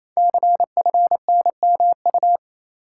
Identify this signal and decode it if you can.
Morse code, heard as a keyed tone, about 28 wpm, 700 Hz. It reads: CFNMU
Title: Code Groups - 5 Letters - 28wpm